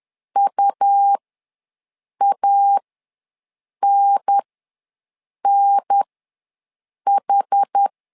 Morse code, heard as a keyed tone, about 10 wpm, spaced as for 6 wpm, 800 Hz. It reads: UANNH
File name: 2 UHDN ws 10wpm